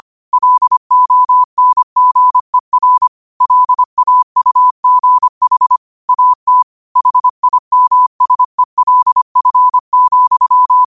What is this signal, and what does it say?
Morse code, heard as a keyed tone, about 25 wpm, 1000 Hz. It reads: LONGER LAUGH AT HIMSELF,
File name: qotdfiles / 25wpm